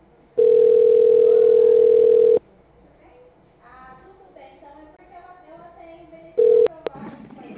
Ringback tone (North America), 440 + 480 Hz.
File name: out- -2018- - - .8571